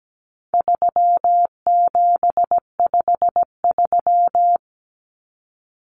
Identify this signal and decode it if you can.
Morse code, heard as a keyed tone, about 17 wpm, 700 Hz. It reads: 3753
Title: Common Cut Numbers - 4-Digit 17wpm